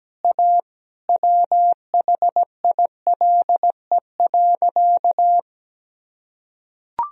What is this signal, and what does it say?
Morse code, heard as a keyed tone, about 17 wpm, 700 Hz. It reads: A WHILE.
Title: Sentences from Top 500 Words and Abbreviations 17wpm